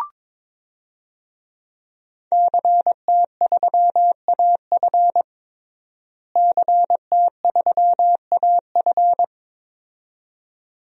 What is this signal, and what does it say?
Morse code, heard as a keyed tone, about 22 wpm, 700 Hz. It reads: CT3AF CT3AF
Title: DXCC-100 Call Signs - 5 Characters - Repeated 2x - 22wpm